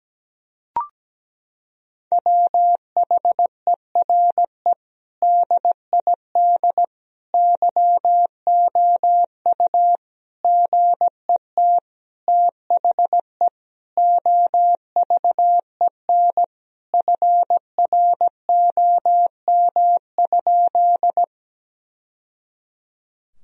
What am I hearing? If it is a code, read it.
Morse code, heard as a keyed tone, about 17 wpm, 700 Hz. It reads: WHERE DID YOU GET THE OVEN FROM?
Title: Sentences from Top 1000 Words - Review Part 4 - 17wpm